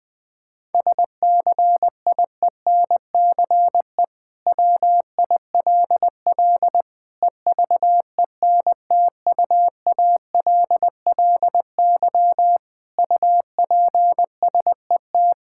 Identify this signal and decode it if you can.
Morse code, heard as a keyed tone, about 20 wpm, 700 Hz. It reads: SCIENCE WILL EVENTUALLY UPSET
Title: Sentences from Top 1000 Words - Review Part 6 - 20wpm